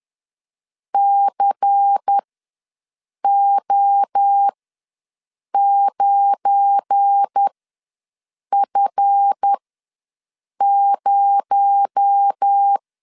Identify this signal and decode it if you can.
Morse code, heard as a keyed tone, about 10 wpm, spaced as for 6 wpm, 800 Hz. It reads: CO9F0
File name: LEZIONI / 9 2 ws 10wpm 4130